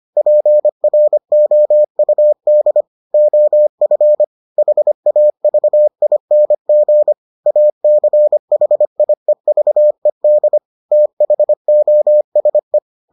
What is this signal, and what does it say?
Morse code, heard as a keyed tone, about 25 wpm, 600 Hz. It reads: PROUD OF HAVING ACHIEVED THOSE